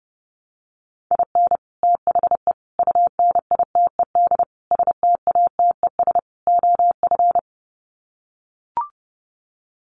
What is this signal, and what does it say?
Morse code, heard as a keyed tone, about 30 wpm, 700 Hz. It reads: IN THE UNITED STATES OF
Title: Sets of 5 Words Encore2 30wpm